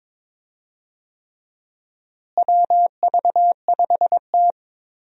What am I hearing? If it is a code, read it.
Morse code, heard as a keyed tone, about 22 wpm, 700 Hz. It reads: WV5T
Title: USA Call Signs - 4 Characters - Repeated 3x - 22wpm